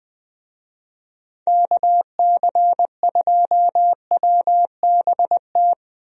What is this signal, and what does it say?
Morse code, heard as a keyed tone, about 21 wpm, 700 Hz. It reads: KC2WBT
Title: USA Call Signs - 6 Characters - Repeated 3x - 20wpm